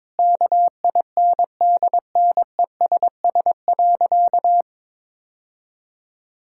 Morse code, heard as a keyed tone, about 21 wpm, 700 Hz. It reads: KINDNESS.
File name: Sentences with 7-Word Phrases - 22wpm